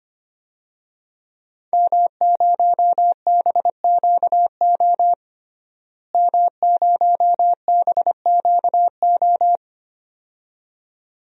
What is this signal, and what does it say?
Morse code, heard as a keyed tone, about 25 wpm, 700 Hz. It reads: M0BQO M0BQO
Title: DXCC-100 Call Signs - 5 Characters - Repeated 2x - 25wpm